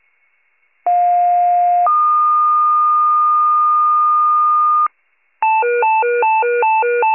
Two-tone page, heard as a one-second tone, then a three-second tone, about 710 then 1220 Hz. It is heard from 0.9 to 4.9 s.
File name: wcalert